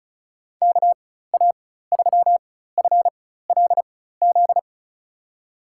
Morse code, heard as a keyed tone, about 36 wpm, spaced as for 16 wpm, 700 Hz. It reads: KA3FLZ
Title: USA Call Signs - 6 Characters - 4x Character Spacing - 35wpm